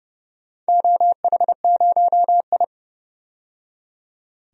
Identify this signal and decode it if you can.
Morse code, heard as a keyed tone, about 30 wpm, 700 Hz. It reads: OH0I
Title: DXCC-100 Call Signs - 4 Characters - 30wpm